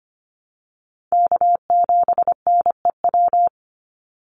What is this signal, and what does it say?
Morse code, heard as a keyed tone, about 25 wpm, 700 Hz. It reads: K7NEW
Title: NA SOTA Chaser Callsigns 25wpm